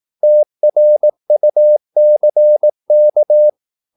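Morse code, heard as a keyed tone, about 18 wpm, 600 Hz. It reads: TRUCK
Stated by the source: Morse practice at 18 wpm, English text, from CW med Gnister.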